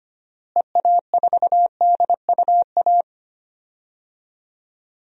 Morse code, heard as a keyed tone, about 25 wpm, 700 Hz. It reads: EA4DUA
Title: DXCC-100 Call Signs - 6 Characters - Repeated 3x - 25wpm